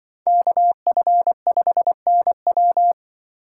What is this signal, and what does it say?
Morse code, heard as a keyed tone, about 24 wpm, 700 Hz. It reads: KF5NW
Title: Speed Racing - USA 2x2 Call Signs 20wpm